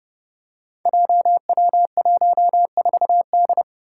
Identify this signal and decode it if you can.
Morse code, heard as a keyed tone, about 30 wpm, 700 Hz. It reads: JW14D
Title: Five Challenging Letter-Number Combinations 30wpm